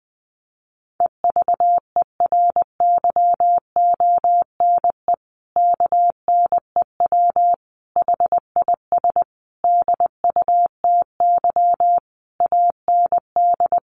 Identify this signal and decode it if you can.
Morse code, heard as a keyed tone, about 20 wpm, 700 Hz. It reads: EVERYONE KNEW HIS DUTY AND
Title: Sentences from Top 1500 Words 20wpm